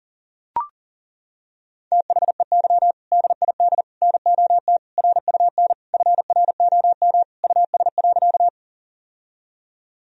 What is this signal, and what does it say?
Morse code, heard as a keyed tone, about 40 wpm, 700 Hz. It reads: THEY DID NOT RUN FROM US.
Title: Sentences from Top 1000 Words - Review Part 1 - 40wpm